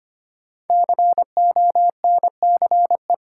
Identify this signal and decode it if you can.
Morse code, heard as a keyed tone, about 25 wpm, 700 Hz. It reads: CONCE
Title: Sentences from Top 1000 Words - New Word Drill - 25wpm